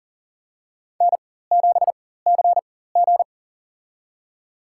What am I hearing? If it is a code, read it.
Morse code, heard as a keyed tone, about 40 wpm, spaced as for 19 wpm, 700 Hz. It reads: N7CG
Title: USA Call Signs - 4 Characters - 4x Character Spacing - 40wpm